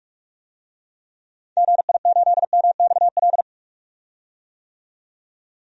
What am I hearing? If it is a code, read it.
Morse code, heard as a keyed tone, about 46 wpm, 700 Hz. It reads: GI8MXL
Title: DXCC-100 Call Signs - 6 Characters - 45wpm